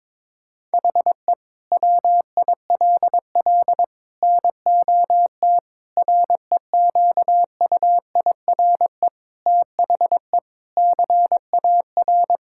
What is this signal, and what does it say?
Morse code, heard as a keyed tone, about 22 wpm, 700 Hz. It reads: HE WILL NOT REQUIRE THE CAR
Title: Sentences from Top 1000 Words - Review Part 6 - 22wpm